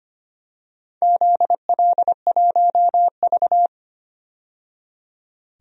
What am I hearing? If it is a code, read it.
Morse code, heard as a keyed tone, about 25 wpm, 700 Hz. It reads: ZL1V
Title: DXCC-100 Call Signs - 4 Characters - 25wpm